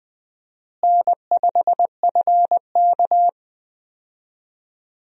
Morse code, heard as a keyed tone, about 20 wpm, 700 Hz. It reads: N5FK
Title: USA Call Signs - 4 Characters - Repeated 2x - 20wpm